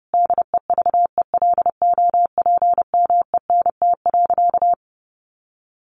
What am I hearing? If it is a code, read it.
Morse code, heard as a keyed tone, about 30 wpm, 700 Hz. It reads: DEVELOPMENT.
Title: Sentences from Top 1400 Words 30wpm